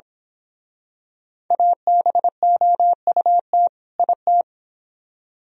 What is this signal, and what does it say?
Morse code, heard as a keyed tone, about 26 wpm, 700 Hz. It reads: ABOUT IT
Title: Speed-Racing - Sets of 2 Words 22wpm